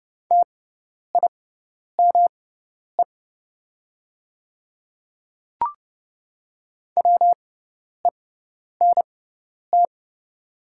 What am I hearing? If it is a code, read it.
Morse code, heard as a keyed tone, about 30 wpm, spaced as for 10 wpm, 700 Hz. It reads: TIME WENT
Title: 6x Character Spacing - Top 300 Words - 30wpm